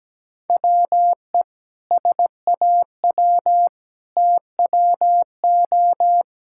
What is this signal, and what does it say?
Morse code, heard as a keyed tone, about 17 wpm, 700 Hz. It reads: WE SAW TWO
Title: Sentences from Top 1000 Words - Review Part 6 - 17wpm